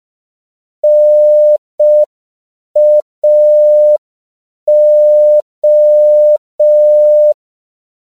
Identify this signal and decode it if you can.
Morse code, heard as a keyed tone, about 5 wpm, 600 Hz. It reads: NAO
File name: CW / 5PPM